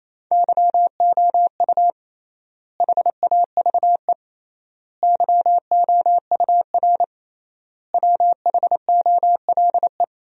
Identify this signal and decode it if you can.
Morse code, heard as a keyed tone, about 28 wpm, 700 Hz. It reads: YOU HAVE YOUR WHOLE
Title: 3x Word-Spacing - Sentences from Top 400 Words - 28wpm